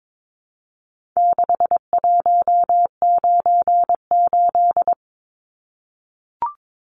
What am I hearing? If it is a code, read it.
Morse code, heard as a keyed tone, about 22 wpm, 700 Hz. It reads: 6198
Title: Numbers - 4-Digit 22wpm